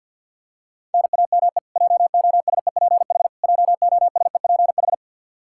Morse code, heard as a keyed tone, about 53 wpm, 700 Hz. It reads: NAME JOSEPH JOSEPH
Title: Practice QSO - Exchange 1 - Beginning - 50wpm